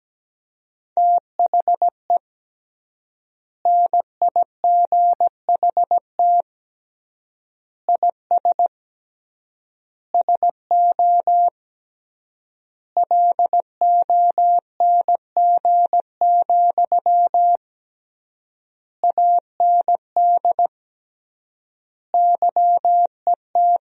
Morse code, heard as a keyed tone, about 17 wpm, 700 Hz. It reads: THE NIGHT IS SO LONG, AND YET
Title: 3x Word-Spacing - Sentences from Top 500 Words - 17wpm